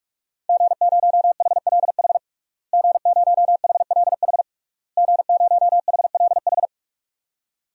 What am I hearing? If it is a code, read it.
Morse code, heard as a keyed tone, about 48 wpm, 700 Hz. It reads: G0HLH G0HLH G0HLH
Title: DXCC-100 Call Signs - 5 Characters - Repeated 3x - 45wpm